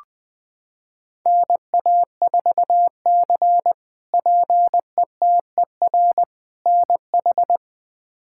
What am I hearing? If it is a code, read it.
Morse code, heard as a keyed tone, about 20 wpm, 700 Hz. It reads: NA4C PETER NH
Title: K1USN SST QSO Practice - 20wpm